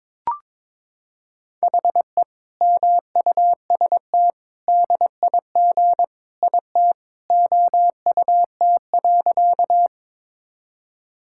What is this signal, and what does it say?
Morse code, heard as a keyed tone, about 22 wpm, 700 Hz. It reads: HE MUST DIG IT OUT.
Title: Sentences from Top 1600 Words 22wpm